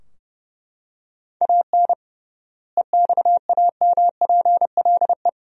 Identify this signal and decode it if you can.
Morse code, heard as a keyed tone, about 32 wpm, 700 Hz. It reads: AN EXAMPLE
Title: 3x Word-Spacing - Sentences from Top 400 Words - 30wpm